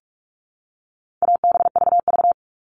Morse code, heard as a keyed tone, about 46 wpm, 700 Hz. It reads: ABVV
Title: All Cut Numbers - 4-Digit 45wpm